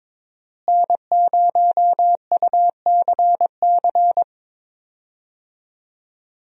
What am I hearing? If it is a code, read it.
Morse code, heard as a keyed tone, about 22 wpm, 700 Hz. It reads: N0UCC
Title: USA Call Signs - 5 Characters - Repeated 2x - 22wpm